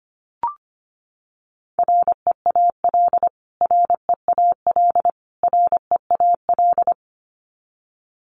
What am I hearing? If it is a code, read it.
Morse code, heard as a keyed tone, about 25 wpm, 700 Hz. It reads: REAL REAL REAL
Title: Top 500 Words Repeated 25wpm